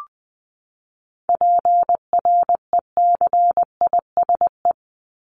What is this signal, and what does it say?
Morse code, heard as a keyed tone, about 20 wpm, 700 Hz. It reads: PRECISE
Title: LY Suffix - Top 100 Words with Base Words 20wpm